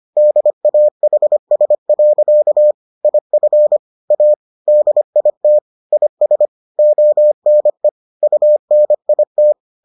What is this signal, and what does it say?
Morse code, heard as a keyed tone, about 25 wpm, 600 Hz. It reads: DAHS. IF A DIT IS ONE UNIT